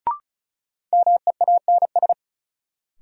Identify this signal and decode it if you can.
Morse code, heard as a keyed tone, about 36 wpm, 700 Hz. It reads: MEANS
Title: Sets of 5 Words - New Word Drill - 35wpm